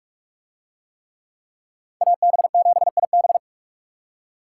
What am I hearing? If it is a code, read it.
Morse code, heard as a keyed tone, about 42 wpm, 700 Hz. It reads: AB7IB